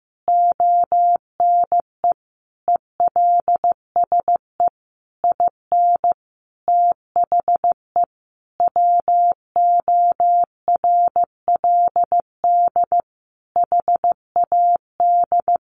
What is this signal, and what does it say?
Morse code, heard as a keyed tone, about 15 wpm, 700 Hz. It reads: ONE ELSE IN THE WORLD HAD
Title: Sentences from Top 1100 Words 15wpm